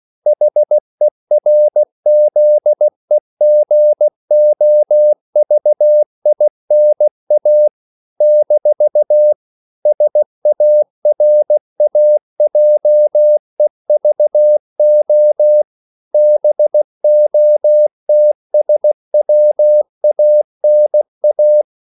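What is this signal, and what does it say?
Morse code, heard as a keyed tone, about 16 wpm, 600 Hz. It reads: HERZEGOVINA - SARAJEVO BOTSWANA
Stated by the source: CW practice of national capitals at 16 words per minute, from ON6ZQ.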